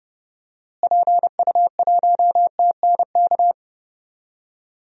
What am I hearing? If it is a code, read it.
Morse code, heard as a keyed tone, about 30 wpm, 700 Hz. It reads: PU1TNK
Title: DXCC-100 Call Signs - 6 Characters - Repeated 3x - 30wpm